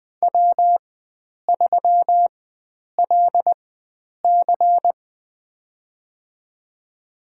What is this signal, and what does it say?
Morse code, heard as a keyed tone, about 20 wpm, spaced as for 9 wpm, 700 Hz. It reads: W3LC
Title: USA Call Signs - 4 Characters - 4x Character Spacing - 20wpm